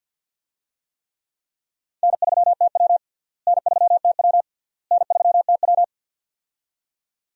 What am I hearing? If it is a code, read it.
Morse code, heard as a keyed tone, about 52 wpm, 700 Hz. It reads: N3TW N3TW N3TW
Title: USA Call Signs - 4 Characters - Repeated 3x - 50wpm